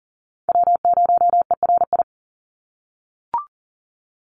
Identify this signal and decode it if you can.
Morse code, heard as a keyed tone, about 39 wpm, 700 Hz. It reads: W0ERI
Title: NA SOTA Chaser Callsigns 40wpm